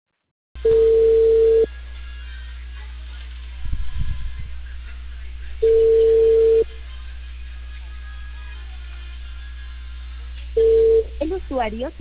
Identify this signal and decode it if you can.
Ringback tone (China, 450 Hz), with speech.